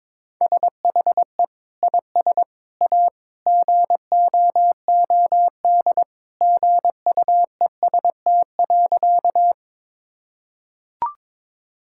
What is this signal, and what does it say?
Morse code, heard as a keyed tone, about 21 wpm, 700 Hz. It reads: SHE IS A GOOD GUEST.
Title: Sentences from Top 1600 Words 22wpm